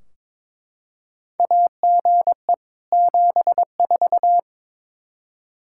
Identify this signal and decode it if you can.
Morse code, heard as a keyed tone, about 22 wpm, 700 Hz. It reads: AGE 74
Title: Practice QSO - Exchange 3 - More Info - 22wpm